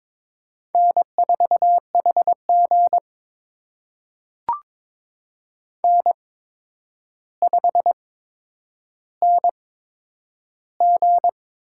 Morse code, heard as a keyed tone, about 22 wpm, spaced as for 9 wpm, 700 Hz. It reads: N4HG N5NG
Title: USA Call Signs - 4 Characters - 8x Character Spacing - 22wpm